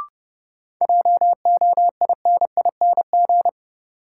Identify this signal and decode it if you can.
Morse code, heard as a keyed tone, about 30 wpm, 700 Hz. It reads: JOINING
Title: Sets of 5 Words - New Word Drill - 30wpm